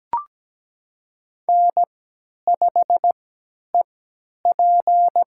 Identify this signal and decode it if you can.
Morse code, heard as a keyed tone, about 17 wpm, spaced as for 10 wpm, 700 Hz. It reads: N5EP
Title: USA Call Signs - 4 Characters - 3x Character Spacing - 17wpm